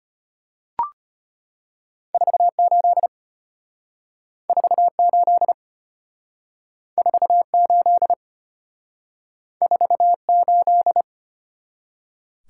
Morse code, heard as a keyed tone, about 33 wpm, 700 Hz. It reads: 48 48 48 48